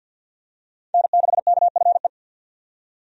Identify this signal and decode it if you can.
Morse code, heard as a keyed tone, about 48 wpm, 700 Hz. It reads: N6KFE
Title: USA Call Signs - 5 Characters - 50wpm